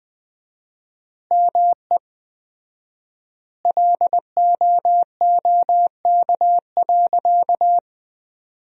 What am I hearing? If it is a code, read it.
Morse code, heard as a keyed tone, about 20 wpm, 700 Hz. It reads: ME LOOK.
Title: 4x Word-Spacing - Sentences from Top 300 Words - 20wpm